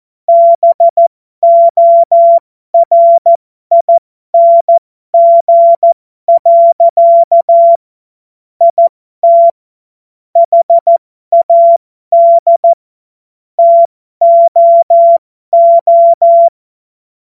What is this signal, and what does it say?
Morse code, heard as a keyed tone, about 14 wpm, spaced as for 12 wpm, 680 Hz. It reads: BORING. IT HAD TOO
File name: Morsekurs / Witze